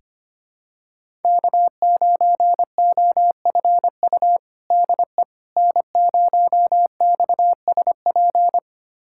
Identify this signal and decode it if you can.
Morse code, heard as a keyed tone, about 25 wpm, 700 Hz. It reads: K9OFU DE N0XSP
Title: USA Callsign DE Callsign - 5 Characters - 25wpm